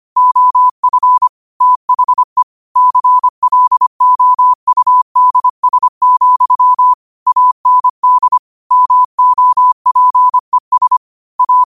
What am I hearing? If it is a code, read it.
Morse code, heard as a keyed tone, about 25 wpm, 1000 Hz. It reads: OF THE CLOUDS, AND MOPES A
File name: qotdfiles / 25wpm